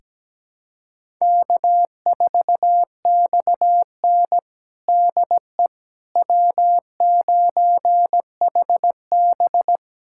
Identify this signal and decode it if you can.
Morse code, heard as a keyed tone, about 17 wpm, 700 Hz. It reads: K4XN DE W9HB
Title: USA Callsign DE Callsign - 4 Characters - 17wpm